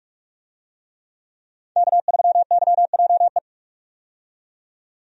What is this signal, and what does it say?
Morse code, heard as a keyed tone, about 46 wpm, 700 Hz. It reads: K3YJE
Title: USA Call Signs - 5 Characters - 45wpm